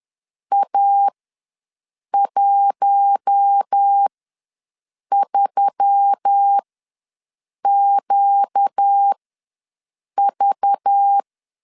Morse code, heard as a keyed tone, about 10 wpm, spaced as for 6 wpm, 800 Hz. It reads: A13QV
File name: LEZIONI / 5 LBVC / 5 LBV ws 10wpm 3645